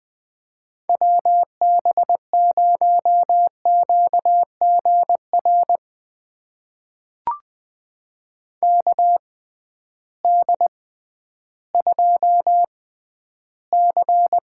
Morse code, heard as a keyed tone, about 20 wpm, spaced as for 12 wpm, 700 Hz. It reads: WB0QGR KD2C
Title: USA Call Signs - 6x Character Spacing - 20wpm